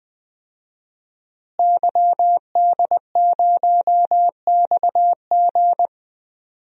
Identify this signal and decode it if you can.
Morse code, heard as a keyed tone, about 20 wpm, 700 Hz. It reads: YD0XG
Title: DXCC-100 Call Signs - 5 Characters - Repeated 2x - 20wpm